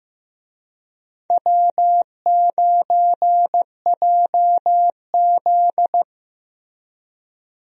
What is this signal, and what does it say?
Morse code, heard as a keyed tone, about 15 wpm, 700 Hz. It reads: W9JZ
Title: USA Call Signs - 4 Characters - Repeated 2x - 15wpm